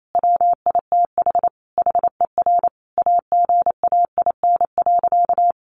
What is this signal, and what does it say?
Morse code, heard as a keyed tone, about 28 wpm, 700 Hz. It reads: WITH HER AGAIN.
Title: Sentences from Top 1500 Words 28wpm